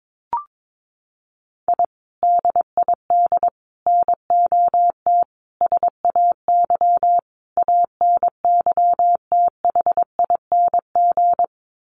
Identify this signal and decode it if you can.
Morse code, heard as a keyed tone, about 22 wpm, 700 Hz. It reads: I DID NOT SAY ANYTHING
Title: Sets of 5 Words Encore2 22wpm